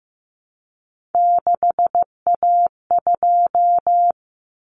Morse code, heard as a keyed tone, about 15 wpm, 700 Hz. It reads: 6A2
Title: Common Cut Numbers - 3-Digit 15wpm